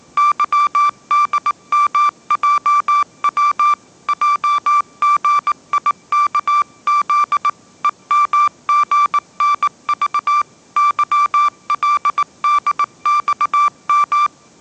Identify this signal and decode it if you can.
Morse code, heard as a keyed tone, about 19 wpm, 1170 Hz. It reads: YDMJW JGIKZ EMGNV YLDXM
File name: mrx083 20wpm